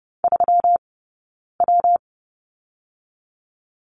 Morse code, heard as a keyed tone, about 30 wpm, 700 Hz. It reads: 3 W
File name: Lesson 20d - Number 3 - Two Characters 30wpm